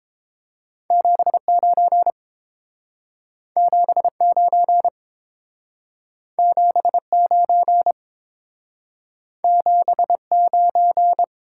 Morse code, heard as a keyed tone, about 30 wpm, 700 Hz. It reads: 79 79 79 79